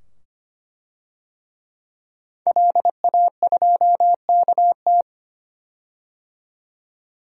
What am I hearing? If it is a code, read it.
Morse code, heard as a keyed tone, about 25 wpm, 700 Hz. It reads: LA2KT
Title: DXCC-100 Call Signs - 5 Characters - 25wpm